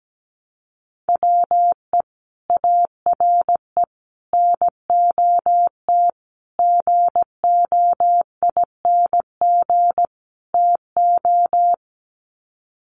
Morse code, heard as a keyed tone, about 17 wpm, 700 Hz. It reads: WE ARE NOT GOING TO
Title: Sets of 5 Words 17wpm